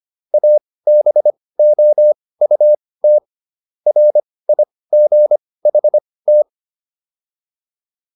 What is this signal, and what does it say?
Morse code, heard as a keyed tone, about 25 wpm, spaced as for 18 wpm, 600 Hz. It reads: ABOUT RIGHT
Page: intermediate-practice-files